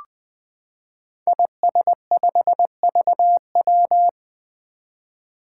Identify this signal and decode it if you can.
Morse code, heard as a keyed tone, about 20 wpm, 700 Hz. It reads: IS5VW
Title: Five Challenging Letter-Number Combinations 20wpm